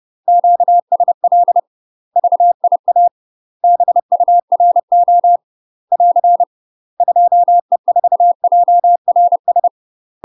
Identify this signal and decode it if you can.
Morse code, heard as a keyed tone, about 30 wpm, 700 Hz. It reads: QSL VIA BURO <AR> 2E4JRS
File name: UK MASTER QSOs TRAINING / 30WPM / UK MASTER QSOs 0048